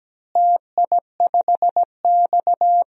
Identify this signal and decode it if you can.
Morse code, heard as a keyed tone, about 17 wpm, 700 Hz. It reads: TI5X
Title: DXCC-100 Call Signs - 4 Characters - Repeated 3x - 17wpm